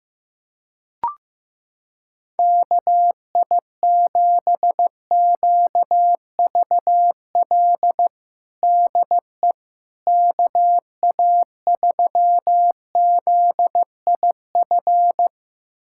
Morse code, heard as a keyed tone, about 15 wpm, 700 Hz. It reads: KI7QVL DE KA3ZIF
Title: USA Callsign DE Callsign 15wpm